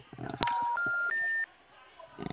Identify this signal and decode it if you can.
Special information tone, with noise.